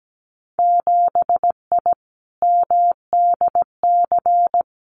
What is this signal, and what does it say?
Morse code, heard as a keyed tone, about 17 wpm, 700 Hz. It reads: 7I MDC
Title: Winter Field Day Exchanges Encore 17wpm